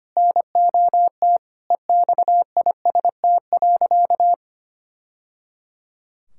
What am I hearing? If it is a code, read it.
Morse code, heard as a keyed tone, about 25 wpm, 700 Hz. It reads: NOT EXIST.
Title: Sentences from Top 1000 Words - Review Part 5 - 25wpm